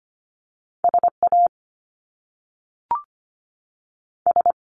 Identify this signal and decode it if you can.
Morse code, heard as a keyed tone, about 25 wpm, 700 Hz. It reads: S A S
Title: Top Two Letter Combinations in Words 25wpm